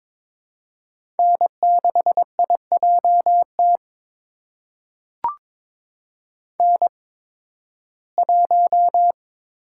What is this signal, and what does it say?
Morse code, heard as a keyed tone, about 22 wpm, spaced as for 14 wpm, 700 Hz. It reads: N6IJT N1
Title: USA Call Signs - 5 Characters - 8x Character Spacing - 22wpm